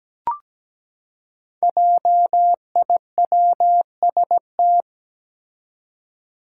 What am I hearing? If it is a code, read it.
Morse code, heard as a keyed tone, about 17 wpm, 700 Hz. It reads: JIWST
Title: Five Challenging Letter-Number Combinations 17wpm